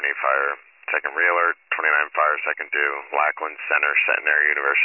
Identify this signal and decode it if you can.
Emergency Alert System (SAME) tones, heard as harsh buzzing bursts, with speech. It is heard from 0.9 to 4.6 s.